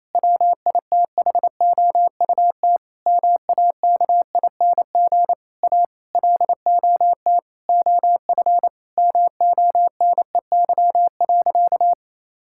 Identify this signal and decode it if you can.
Morse code, heard as a keyed tone, about 28 wpm, 700 Hz. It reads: WITHOUT MAKING A LOT OF MONEY.
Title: Sentences from Top 1000 Words - Review Part 6 - 28wpm